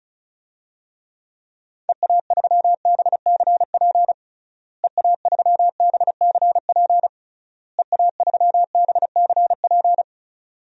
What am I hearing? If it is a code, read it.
Morse code, heard as a keyed tone, about 36 wpm, 700 Hz. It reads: EA3BCP EA3BCP EA3BCP
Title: DXCC-100 Call Signs - 6 Characters - Repeated 3x - 35wpm